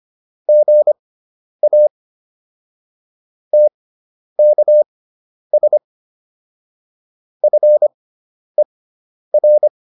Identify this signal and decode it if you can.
Morse code, heard as a keyed tone, about 25 wpm, spaced as for 10 wpm, 600 Hz. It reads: GA TKS FER
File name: QSO401 10